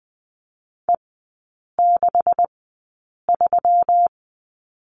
Morse code, heard as a keyed tone, about 20 wpm, spaced as for 8 wpm, 700 Hz. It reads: E63
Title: 218 Lesson 35e - Number 0 - Three Characters 20wpm